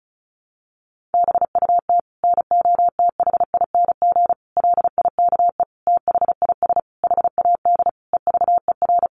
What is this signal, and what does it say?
Morse code, heard as a keyed tone, about 34 wpm, 700 Hz. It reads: BUT NOTHING LIKE THIS HAD EVER
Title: Sentences from Top 1100 Words 35wpm